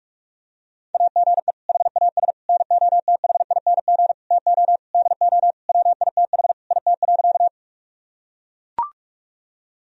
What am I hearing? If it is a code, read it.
Morse code, heard as a keyed tone, about 42 wpm, 700 Hz. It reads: AGE HAS NOTHING TO DO WITH IT.
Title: Sentences from Top 1000 Words - Review Part 3 - 45wpm